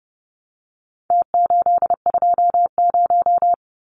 Morse code, heard as a keyed tone, about 30 wpm, 700 Hz. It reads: T820
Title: Numbers - Contest Style - 4-Digit 30wpm